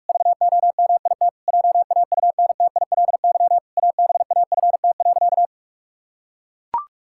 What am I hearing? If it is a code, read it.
Morse code, heard as a keyed tone, about 41 wpm, 700 Hz. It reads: VOMIT JAUNTILY ABAFT.
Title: Pangram Challenge v2 - 45wpm